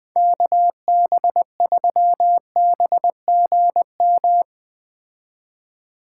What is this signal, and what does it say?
Morse code, heard as a keyed tone, about 20 wpm, 700 Hz. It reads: KB3BGM
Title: Speed Racing - USA 2x3 Call Signs 17wpm